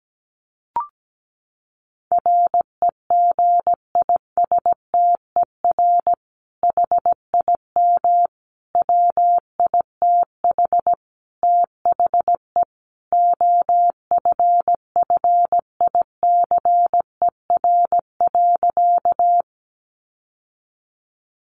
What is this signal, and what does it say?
Morse code, heard as a keyed tone, about 17 wpm, 700 Hz. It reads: REGISTER HIM WITH THE OFFICER.
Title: Sentences from Top 1200 Words 17wpm